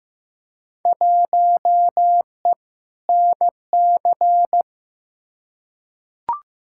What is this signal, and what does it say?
Morse code, heard as a keyed tone, about 15 wpm, 700 Hz. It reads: 1E NC
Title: Field Day Exchanges - 15wpm